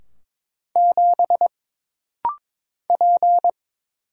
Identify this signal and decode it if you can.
Morse code, heard as a keyed tone, about 23 wpm, 700 Hz. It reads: 7 P
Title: Single Letter-Number Alt - Rapid-Fire 22wpm